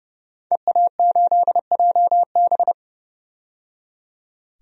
Morse code, heard as a keyed tone, about 30 wpm, 700 Hz. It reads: EA8JB
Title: DXCC-100 Call Signs - 5 Characters - 30wpm